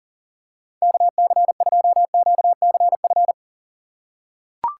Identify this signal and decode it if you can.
Morse code, heard as a keyed tone, about 40 wpm, 700 Hz. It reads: KC2QCF
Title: USA Call Signs - 6 Characters - 6x Character Spacing - 40wpm